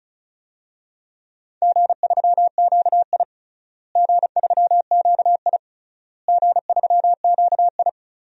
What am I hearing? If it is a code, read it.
Morse code, heard as a keyed tone, about 34 wpm, 700 Hz. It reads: G3QI G3QI G3QI
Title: DXCC-100 Call Signs - 4 Characters - Repeated 3x - 35wpm